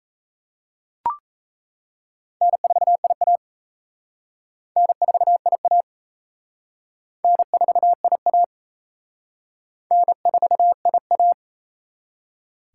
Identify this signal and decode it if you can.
Morse code, heard as a keyed tone, about 33 wpm, 700 Hz. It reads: N4IA N4IA N4IA N4IA
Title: Speed Racing - USA Call Signs 28wpm